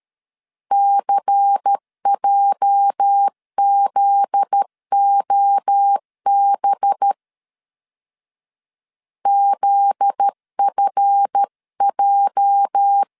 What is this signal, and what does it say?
Morse code, heard as a keyed tone, about 13 wpm, 800 Hz. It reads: CJZOB ZFJ
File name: LEZIONI / 8 YJ45 / 8 YJ 12wpm 2235